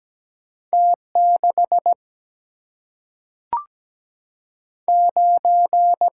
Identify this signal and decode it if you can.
Morse code, heard as a keyed tone, about 17 wpm, 700 Hz. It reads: T 6 9
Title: Single-Letter Single-Number 17wpm